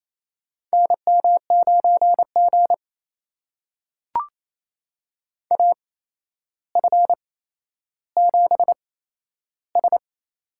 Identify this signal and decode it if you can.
Morse code, heard as a keyed tone, about 28 wpm, spaced as for 12 wpm, 700 Hz. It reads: NM9G AF7S
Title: USA Call Signs - 4 Characters - 8x Character Spacing - 28wpm